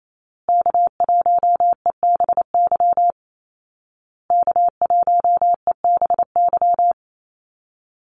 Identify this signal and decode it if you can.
Morse code, heard as a keyed tone, about 28 wpm, 700 Hz. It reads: K1EBY K1EBY
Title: Call Signs Repeated 2x - 28wpm